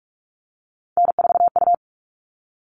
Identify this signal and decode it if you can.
Morse code, heard as a keyed tone, about 44 wpm, 700 Hz. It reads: N4U
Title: All Cut Numbers - 3-Digit 45wpm